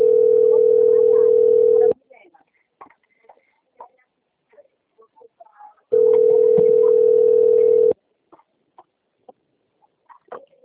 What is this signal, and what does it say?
Ringback tone (North America), 440 + 480 Hz, with speech.
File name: rg-1600-unknown- - - .23864